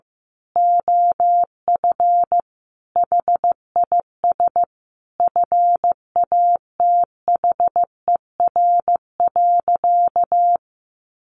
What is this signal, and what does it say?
Morse code, heard as a keyed tone, about 15 wpm, 700 Hz. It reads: OF HIS FATHER.
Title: Sentences from Top 1500 Words 15wpm